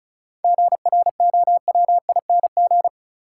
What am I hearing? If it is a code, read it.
Morse code, heard as a keyed tone, about 36 wpm, 700 Hz. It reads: GROWING
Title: Sets of 5 Words Encore - New Word Drill - 35wpm